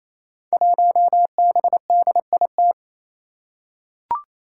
Morse code, heard as a keyed tone, about 27 wpm, 700 Hz. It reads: 1BDIT
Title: Five Challenging Letter-Number Combinations 28wpm